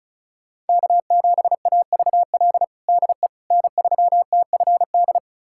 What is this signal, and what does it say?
Morse code, heard as a keyed tone, about 36 wpm, 700 Hz. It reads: K7AVL DE N3TFD
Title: USA Callsign DE Callsign - 5 Characters - 35wpm